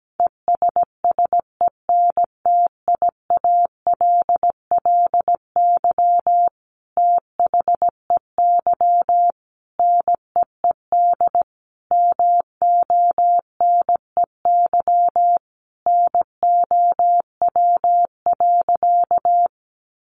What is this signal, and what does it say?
Morse code, heard as a keyed tone, about 17 wpm, 700 Hz. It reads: ESSENTIALLY THEY NEED MONEY NOW.
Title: Sentences from Top 1300 Words 17wpm